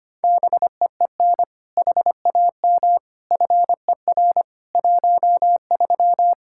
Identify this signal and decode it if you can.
Morse code, heard as a keyed tone, about 25 wpm, 700 Hz. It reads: BEEN HAM FER 13
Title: Practice QSO - Exchange 3 - More Info - 25wpm